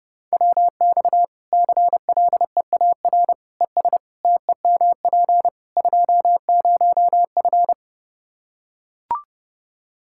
Morse code, heard as a keyed tone, about 32 wpm, 700 Hz. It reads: WX CLEAR ES TEMP 20F
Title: Practice QSO - Exchange 2 - Info - 30wpm